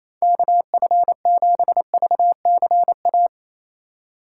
Morse code, heard as a keyed tone, about 28 wpm, 700 Hz. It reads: KF7VCA
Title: USA Call Signs - 6 Characters - 4x Character Spacing - 28wpm